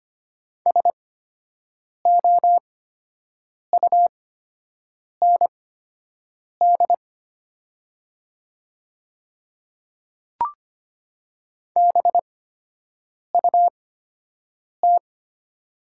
Morse code, heard as a keyed tone, about 25 wpm, spaced as for 7 wpm, 700 Hz. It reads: SOUND BUT
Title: 8x Character Spacing - Top 100 Words - 25wpm